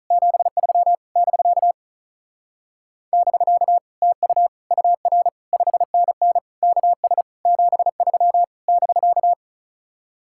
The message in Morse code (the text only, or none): 73 <BK> <BK> TU UR 5NN KS 73 <BK>